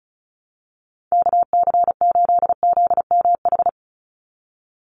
Morse code, heard as a keyed tone, about 34 wpm, 700 Hz. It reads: KC8ZMH
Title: USA Call Signs 35wpm